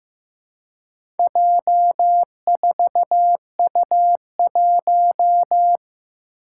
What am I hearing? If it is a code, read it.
Morse code, heard as a keyed tone, about 15 wpm, 700 Hz. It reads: J4U1
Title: Four Challenging Letter-Number Combinations 15wpm